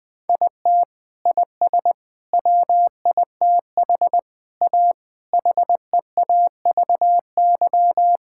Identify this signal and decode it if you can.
Morse code, heard as a keyed tone, about 20 wpm, 700 Hz. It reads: IT IS WITH A HEAVY
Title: Sentences with 7-Word Phrases - Spoken Beginning - 20wpm